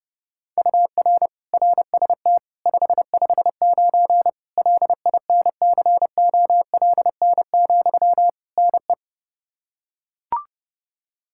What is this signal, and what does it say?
Morse code, heard as a keyed tone, about 30 wpm, 700 Hz. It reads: UR RST 559 LINCOLN, NE
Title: Practice QSO - Exchange 1 - Beginning - 30wpm